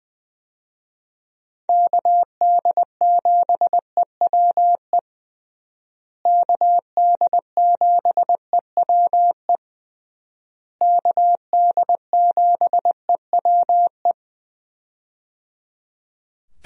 Morse code, heard as a keyed tone, about 20 wpm, 700 Hz. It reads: KD7EWE KD7EWE KD7EWE
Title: USA Call Signs - 6 Characters - Repeated 3x - 20wpm